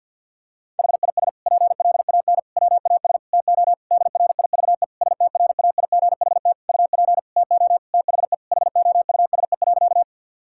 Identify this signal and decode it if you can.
Morse code, heard as a keyed tone, about 50 wpm, 700 Hz. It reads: HIS PLAN WAS TO DRIVE STRAIGHT UP TO THE HOUSE.
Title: Sentences from Top 1000 Words - Review Part 6 - 50wpm